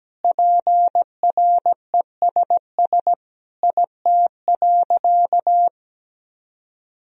Morse code, heard as a keyed tone, about 17 wpm, 700 Hz. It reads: PRESS IT.
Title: Sentences from Top 1000 Words - Review Part 1 - 17wpm